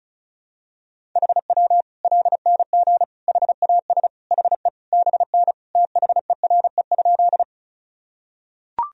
Morse code, heard as a keyed tone, about 34 wpm, 700 Hz. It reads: HW LNG HAS HE BN THERE?
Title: Sentences from Top 100 Words and Abbreviations 35wpm